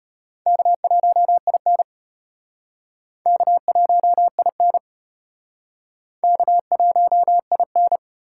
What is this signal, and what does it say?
Morse code, heard as a keyed tone, about 37 wpm, 700 Hz. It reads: K1IN K1IN K1IN